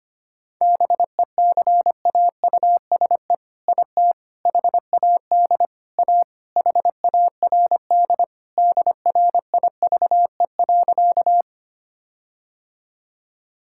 Morse code, heard as a keyed tone, about 25 wpm, 700 Hz. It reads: BECAUSE IT HAD A HARD DRIVE.
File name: practice / 25wpm / All-Age Appropriate Jokes - 25wpm